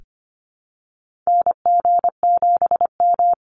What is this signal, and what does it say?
Morse code, heard as a keyed tone, about 25 wpm, 700 Hz. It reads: NG7M
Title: Call Signs Repeated 2x - 25wpm